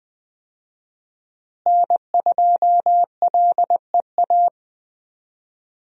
Morse code, heard as a keyed tone, about 20 wpm, 700 Hz. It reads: N2LEA
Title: USA Call Signs - 5 Characters - Repeated 2x - 20wpm